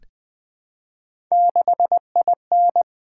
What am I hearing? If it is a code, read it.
Morse code, heard as a keyed tone, about 20 wpm, 700 Hz. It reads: 6IN
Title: Three Letter-Number Combinations - 20wpm